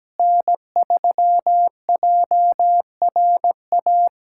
Morse code, heard as a keyed tone, about 17 wpm, 700 Hz. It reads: N3JRA
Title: USA Callsign DE Callsign - 5 Characters - 17wpm